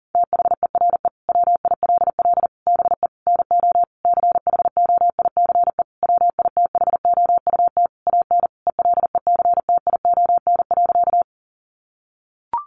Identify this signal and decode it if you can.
Morse code, heard as a keyed tone, about 40 wpm, 700 Hz. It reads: THERE WILL BE NO CHOICE WITHOUT AN ELECTION.
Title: Sentences from Top 1500 Words 40wpm